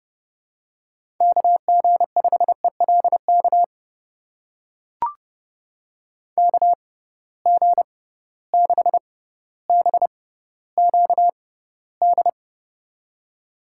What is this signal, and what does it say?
Morse code, heard as a keyed tone, about 30 wpm, spaced as for 15 wpm, 700 Hz. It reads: KG5ELK KG6BQD
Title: USA Call Signs - 6 Characters - 6x Character Spacing - 30wpm